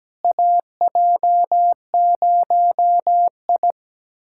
Morse code, heard as a keyed tone, about 17 wpm, 700 Hz. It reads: AJ0I
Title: USA Callsign DE Callsign - 4 Characters - 17wpm